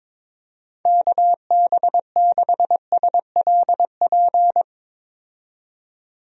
Morse code, heard as a keyed tone, about 22 wpm, 700 Hz. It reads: KB6SLP
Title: USA Call Signs - 6 Characters - 22wpm